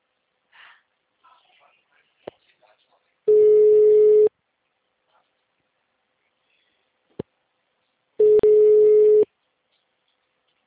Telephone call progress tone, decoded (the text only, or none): ringback tone